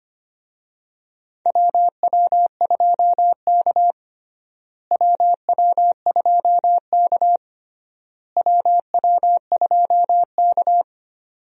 Morse code, heard as a keyed tone, about 25 wpm, 700 Hz. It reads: WW2K WW2K WW2K
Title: USA Call Signs - 4 Characters - Repeated 3x - 25wpm